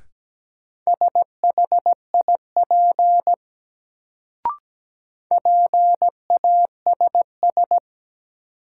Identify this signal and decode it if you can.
Morse code, heard as a keyed tone, about 17 wpm, 700 Hz. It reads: SHIP PASS
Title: Sentences from Top 400 Words - New Word Drill - 17wpm